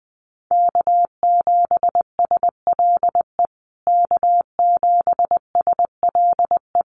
Morse code, heard as a keyed tone, about 20 wpm, 700 Hz. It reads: K7SLE K7SLE
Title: USA Call Signs Repeated 3x 20wpm